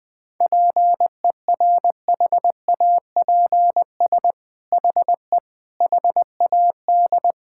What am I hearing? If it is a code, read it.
Morse code, heard as a keyed tone, about 20 wpm, 700 Hz. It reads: PERHAPS HE HAD
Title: Sentences from Top 1600 Words 20wpm